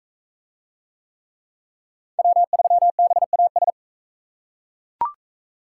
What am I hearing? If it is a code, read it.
Morse code, heard as a keyed tone, about 40 wpm, 700 Hz. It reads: W3BAS